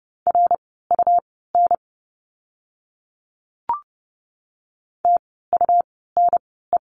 Morse code, heard as a keyed tone, about 30 wpm, spaced as for 17 wpm, 700 Hz. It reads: RUN TUNE
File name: Lesson 17f - Letter u - Words 30wpm